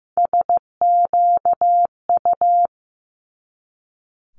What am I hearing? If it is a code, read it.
Morse code, heard as a keyed tone, about 15 wpm, 700 Hz. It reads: SQU
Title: Multi-letter Phonemes 15wpm